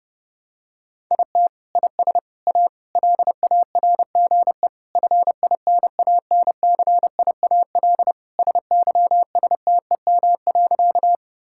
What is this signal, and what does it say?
Morse code, heard as a keyed tone, about 32 wpm, 700 Hz. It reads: IT IS A LARGE FINANCIAL SYSTEM.
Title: Sentences from Top 1000 Words - Review Part 5 - 30wpm